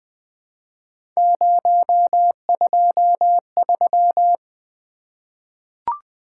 Morse code, heard as a keyed tone, about 20 wpm, 700 Hz. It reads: 023
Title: Three Letter-Number Combinations - 20wpm